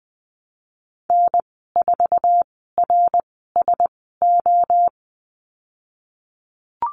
Morse code, heard as a keyed tone, about 20 wpm, spaced as for 15 wpm, 700 Hz. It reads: N4RSO
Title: 069 Lesson 12g - Letter d - Call Signs 20wpm